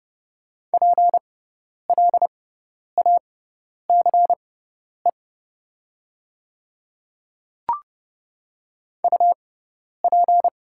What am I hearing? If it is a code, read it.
Morse code, heard as a keyed tone, about 30 wpm, spaced as for 10 wpm, 700 Hz. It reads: PLACE UP
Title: 6x Character Spacing - Top 200 Words - 30wpm